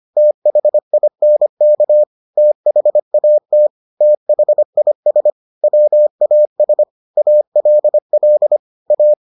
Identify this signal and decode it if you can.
Morse code, heard as a keyed tone, about 25 wpm, 600 Hz. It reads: THINK THAT THIS WAS ALL A